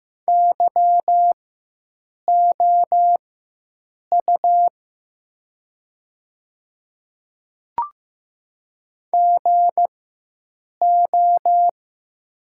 Morse code, heard as a keyed tone, about 15 wpm, spaced as for 7 wpm, 700 Hz. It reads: YOU GO
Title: 4x Character Spacing - Top 200 Words - 15wpm